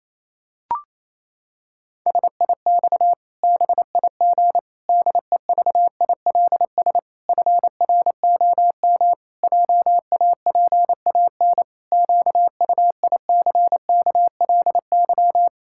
Morse code, heard as a keyed tone, about 28 wpm, 700 Hz. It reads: SIX BIG DEVILS FROM JAPAN QUICKLY
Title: Pangram Challenge v2 - 28wpm